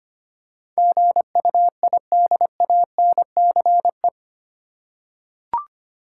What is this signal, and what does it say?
Morse code, heard as a keyed tone, about 25 wpm, 700 Hz. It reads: GUIDANCE
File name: Eight Letter Words - 25wpm